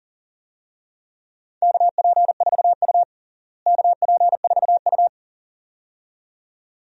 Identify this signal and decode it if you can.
Morse code, heard as a keyed tone, about 40 wpm, 700 Hz. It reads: KP4U KP4U
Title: DXCC-100 Call Signs - 4 Characters - Repeated 2x - 40wpm